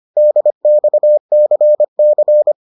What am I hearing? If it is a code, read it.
Morse code, heard as a keyed tone, about 25 wpm, 600 Hz. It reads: DXCC